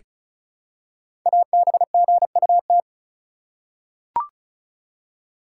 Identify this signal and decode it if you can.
Morse code, heard as a keyed tone, about 36 wpm, 700 Hz. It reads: ABGUT
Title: Code Groups - 5 Letters - 35wpm